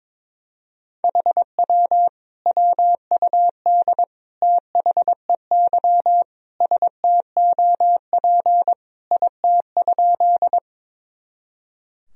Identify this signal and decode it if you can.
Morse code, heard as a keyed tone, about 22 wpm, 700 Hz. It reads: HW WUD THEY STOP IT?
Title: Sentences from Top 300 Words and Abbreviations 22wpm